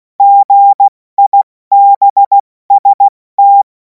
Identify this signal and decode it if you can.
Morse code, heard as a keyed tone, about 16 wpm, 800 Hz. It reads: GIBST
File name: Morsekurs / Witze / Witz 30. Juni 2026 70 F80